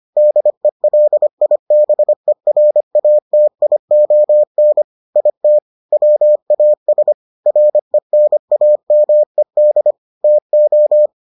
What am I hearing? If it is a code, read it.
Morse code, heard as a keyed tone, about 25 wpm, 600 Hz. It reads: DELIBERATION IT WAS RENAMED TO